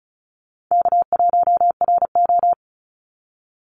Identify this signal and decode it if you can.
Morse code, heard as a keyed tone, about 34 wpm, 700 Hz. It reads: K1RO
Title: Call Signs Repeated 3x - 35wpm